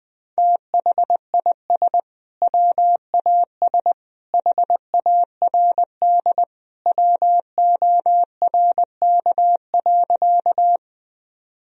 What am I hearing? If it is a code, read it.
Morse code, heard as a keyed tone, about 21 wpm, 700 Hz. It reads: THIS WAS HARD WORK.
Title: Sentences from Top 1000 Words - Review Part 1 - 20wpm